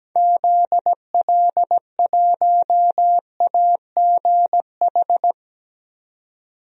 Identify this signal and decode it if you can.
Morse code, heard as a keyed tone, about 17 wpm, 700 Hz. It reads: ZL1AGH
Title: DXCC-100 Call Signs - 6 Characters - 17wpm